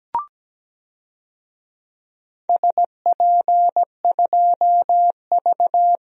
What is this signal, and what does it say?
Morse code, heard as a keyed tone, about 17 wpm, 700 Hz. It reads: SP2V
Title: DXCC-100 Call Signs - 4 Characters - 17wpm